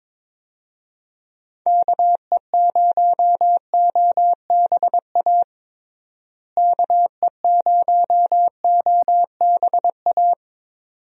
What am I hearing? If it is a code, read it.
Morse code, heard as a keyed tone, about 22 wpm, 700 Hz. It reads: KE0OBA KE0OBA
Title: USA Call Signs - 6 Characters - Repeated 2x - 22wpm